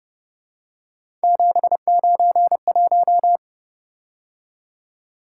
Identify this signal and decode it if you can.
Morse code, heard as a keyed tone, about 29 wpm, 700 Hz. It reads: 791